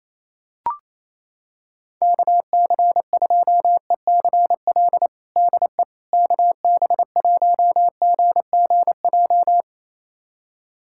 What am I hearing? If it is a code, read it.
Morse code, heard as a keyed tone, about 28 wpm, 700 Hz. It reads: KC2ECL DE KB1GGJ
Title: USA Callsign DE Callsign - 6 Characters - 28wpm